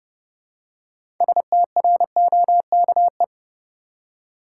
Morse code, heard as a keyed tone, about 30 wpm, 700 Hz. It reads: STROKE
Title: Six Letter Words - 30wpm